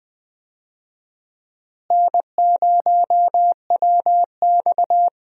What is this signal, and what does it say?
Morse code, heard as a keyed tone, about 20 wpm, 700 Hz. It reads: N0WX
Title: USA Call Signs - 4 Characters - Repeated 3x - 20wpm